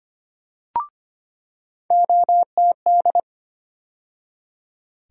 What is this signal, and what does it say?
Morse code, heard as a keyed tone, about 25 wpm, 700 Hz. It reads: OTD
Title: USA Call Sign Postfixes - 3 Letters 25wpm